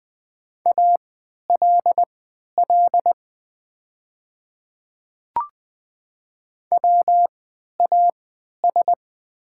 Morse code, heard as a keyed tone, about 20 wpm, spaced as for 11 wpm, 700 Hz. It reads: ALL WAS
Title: 3x Character Spacing - Top 200 Words - 20wpm